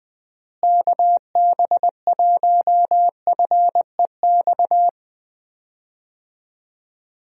Morse code, heard as a keyed tone, about 20 wpm, 700 Hz. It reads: KB1FEX
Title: USA Call Signs - 6 Characters - Repeated 2x - 20wpm